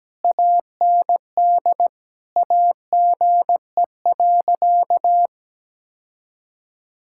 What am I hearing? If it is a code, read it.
Morse code, heard as a keyed tone, about 17 wpm, 700 Hz. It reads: AND AGE.
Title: Sentences from Top 1600 Words 17wpm